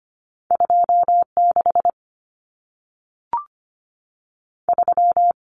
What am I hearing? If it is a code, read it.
Morse code, heard as a keyed tone, about 25 wpm, 700 Hz. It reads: 26 3
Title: Numbers - 2-Digit 25wpm